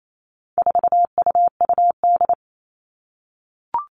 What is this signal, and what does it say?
Morse code, heard as a keyed tone, about 28 wpm, 700 Hz. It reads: 4UUD
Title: All Cut Numbers - 4-Digit 28wpm